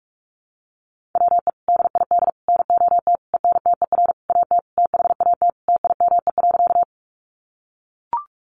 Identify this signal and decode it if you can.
Morse code, heard as a keyed tone, about 44 wpm, 700 Hz. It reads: WE DID NOT ENTER AT THAT TIME.
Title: Sentences from Top 1300 Words 45wpm